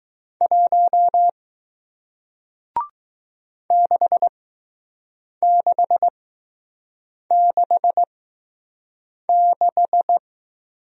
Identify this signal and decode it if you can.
Morse code, heard as a keyed tone, about 20 wpm, spaced as for 6 wpm, 700 Hz. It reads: 1 6666